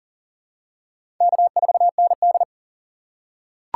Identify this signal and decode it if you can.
Morse code, heard as a keyed tone, about 40 wpm, 700 Hz. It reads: K4ND
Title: USA Call Signs - 4 Characters - 3x Character Spacing - 40wpm